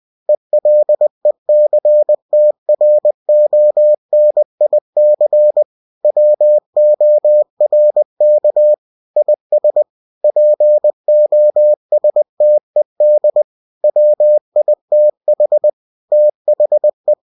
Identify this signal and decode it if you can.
Morse code, heard as a keyed tone, about 20 wpm, 600 Hz. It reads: ELECTRONIC WORK IS POSTED WITH THE
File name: winnie-20wpm